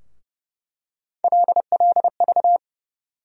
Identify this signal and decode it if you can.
Morse code, heard as a keyed tone, about 32 wpm, 700 Hz. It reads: LLV
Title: USA Call Sign Postfixes - 3 Letters 30wpm